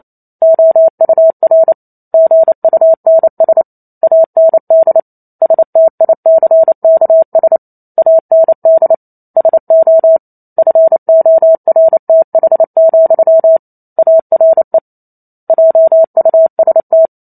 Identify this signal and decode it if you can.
Morse code, heard as a keyed tone, about 28 wpm, 650 Hz. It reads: OUR GUNS AND STICKS AND SO FORTH, ARE JUST